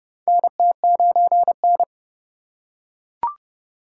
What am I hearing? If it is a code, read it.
Morse code, heard as a keyed tone, about 30 wpm, 700 Hz. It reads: NT9N
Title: USA Call Signs - 3x Character Spacing - 30wpm